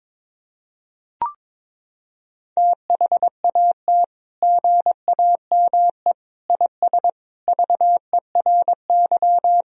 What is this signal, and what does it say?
Morse code, heard as a keyed tone, about 22 wpm, 700 Hz. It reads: THAT GAME IS VERY
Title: Sentences from Top 1000 Words - Review Part 3 - 22wpm